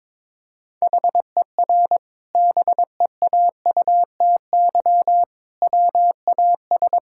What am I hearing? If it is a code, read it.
Morse code, heard as a keyed tone, about 22 wpm, 700 Hz. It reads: HER BEAUTY WAS
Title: Sentences from Top 1000 Words - Review Part 6 - 22wpm